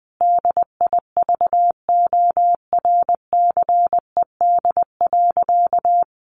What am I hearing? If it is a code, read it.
Morse code, heard as a keyed tone, about 20 wpm, 700 Hz. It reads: DIVORCED.
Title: Additional Easy Phrases 20wpm